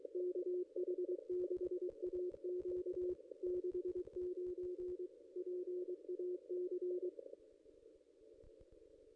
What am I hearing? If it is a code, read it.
Morse code, heard as a keyed tone, about 23 wpm, 360 Hz. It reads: KH6AQ 69 PAC
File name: contests / ss-cw-k5tr-op-n5rz -1900